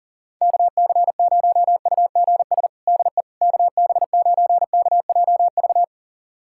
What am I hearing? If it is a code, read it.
Morse code, heard as a keyed tone, about 37 wpm, 700 Hz. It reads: KC0UGS DE KB9KJV
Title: USA Callsign DE Callsign - 6 Characters - 40wpm